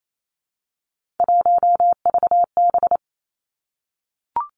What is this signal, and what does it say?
Morse code, heard as a keyed tone, about 28 wpm, 700 Hz. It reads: 1VB
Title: Three Challenging Letter-Number Combinations 28wpm